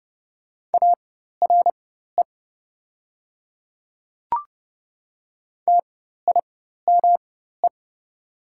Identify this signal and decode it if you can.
Morse code, heard as a keyed tone, about 30 wpm, spaced as for 14 wpm, 700 Hz. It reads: ARE TIME
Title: 4x Character Spacing - Top 100 Words - 30wpm